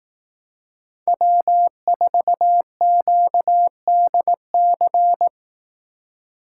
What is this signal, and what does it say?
Morse code, heard as a keyed tone, about 18 wpm, 700 Hz. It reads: W4QDC
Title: Speed Racing - USA 1x3 Call Signs 15wpm